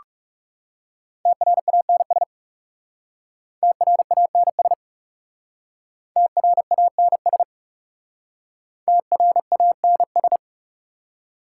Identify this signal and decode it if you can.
Morse code, heard as a keyed tone, about 38 wpm, 690 Hz. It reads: TRANS TRANS TRANS TRANS